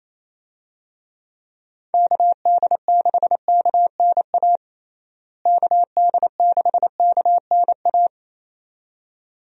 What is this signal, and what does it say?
Morse code, heard as a keyed tone, about 28 wpm, 700 Hz. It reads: KD6KNA KD6KNA
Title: USA Call Signs - 6 Characters - Repeated 2x - 28wpm